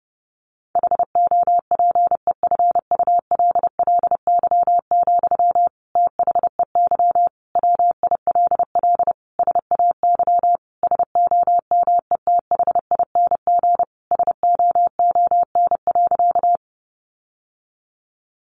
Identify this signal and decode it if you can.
Morse code, heard as a keyed tone, about 30 wpm, 700 Hz. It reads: HOPEFULLY, THEY WILL SAY SOMETHING SOON.
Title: Sentences from Top 1500 Words 30wpm